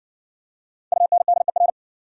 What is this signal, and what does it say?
Morse code, heard as a keyed tone, about 60 wpm, 700 Hz. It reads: UNDER